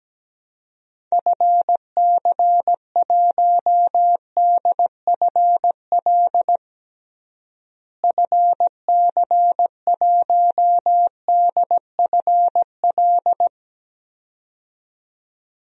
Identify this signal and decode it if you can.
Morse code, heard as a keyed tone, about 17 wpm, 700 Hz. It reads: FC1DFL FC1DFL
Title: DXCC-100 Call Signs - 6 Characters - Repeated 2x - 17wpm